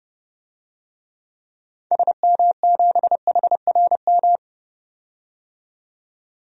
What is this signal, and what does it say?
Morse code, heard as a keyed tone, about 32 wpm, 700 Hz. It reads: SM7HRM
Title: DXCC-100 Call Signs - 6 Characters - 30wpm